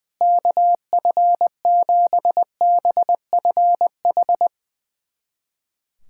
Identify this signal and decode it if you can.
Morse code, heard as a keyed tone, about 20 wpm, 700 Hz. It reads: KF7BFH
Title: USA Callsign DE Callsign 20wpm